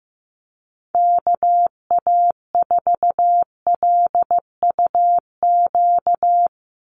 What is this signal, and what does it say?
Morse code, heard as a keyed tone, about 15 wpm, 700 Hz. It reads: KA4LUQ
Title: USA Call Signs 15wpm